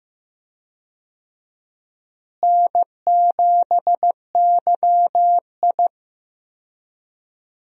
Morse code, heard as a keyed tone, about 16 wpm, 700 Hz. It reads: N7YI
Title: USA Call Signs - 4 Characters - Repeated 2x - 15wpm